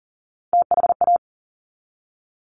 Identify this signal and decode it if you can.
Morse code, heard as a keyed tone, about 38 wpm, 700 Hz. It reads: THA
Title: Top Three Letter Combinations in Words 40wpm